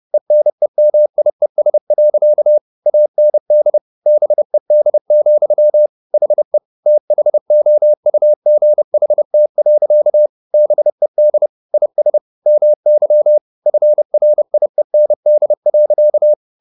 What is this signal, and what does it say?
Morse code, heard as a keyed tone, about 30 wpm, 600 Hz. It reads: ENEMIES. AND BED, HE THOUGHT. BED IS MY FRIEND.